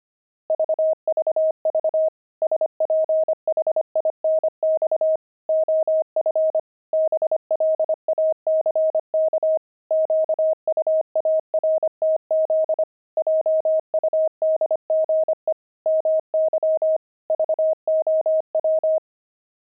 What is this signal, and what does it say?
Morse code, heard as a keyed tone, about 25 wpm, 630 Hz. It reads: VVV SPHINX OF BLACK QUARTZ JUDGE MY VOW
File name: pangram-fs-25-cs-25-sphinx